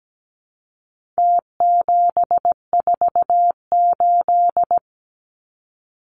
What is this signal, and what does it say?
Morse code, heard as a keyed tone, about 17 wpm, 700 Hz. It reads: T748
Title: Numbers - Contest Style - 4-Digit 17wpm